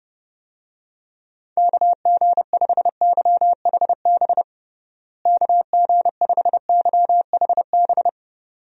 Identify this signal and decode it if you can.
Morse code, heard as a keyed tone, about 30 wpm, 700 Hz. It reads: KG5YHB KG5YHB
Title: USA Call Signs - 6 Characters - Repeated 2x - 30wpm